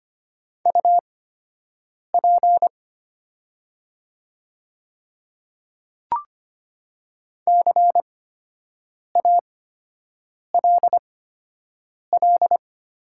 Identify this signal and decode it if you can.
Morse code, heard as a keyed tone, about 25 wpm, spaced as for 7 wpm, 700 Hz. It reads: UP CALL
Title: 8x Character Spacing - Top 200 Words - 25wpm